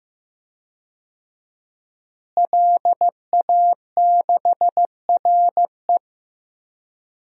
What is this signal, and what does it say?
Morse code, heard as a keyed tone, about 15 wpm, 700 Hz. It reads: LA6RE
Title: DXCC-100 Call Signs - 5 Characters - Repeated 3x - 15wpm